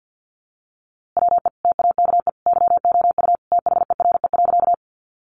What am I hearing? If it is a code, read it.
Morse code, heard as a keyed tone, about 50 wpm, 700 Hz. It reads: WE TAKE YOU THERE.
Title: A Sound Of Thunder 50wpm